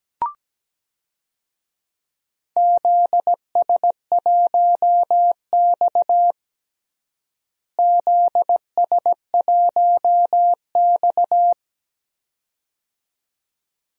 Morse code, heard as a keyed tone, about 17 wpm, 700 Hz. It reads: ZS1X ZS1X
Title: DXCC-100 Call Signs - 4 Characters - Repeated 2x - 17wpm